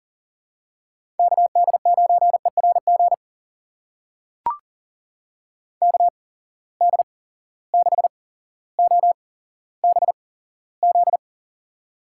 Morse code, heard as a keyed tone, about 40 wpm, spaced as for 17 wpm, 700 Hz. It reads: KD9ERG KD6OBZ
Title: USA Call Signs - 6 Characters - 8x Character Spacing - 40wpm